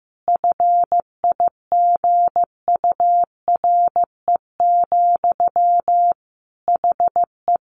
Morse code, heard as a keyed tone, about 15 wpm, 700 Hz. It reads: FIGURE, HE
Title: Sentences from Top 1400 Words 15wpm